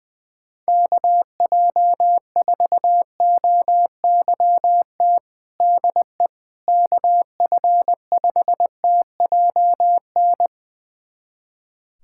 Morse code, heard as a keyed tone, about 21 wpm, 700 Hz. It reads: KJ4OYT DE KF5TJN
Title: USA Callsign DE Callsign - 6 Characters - 20wpm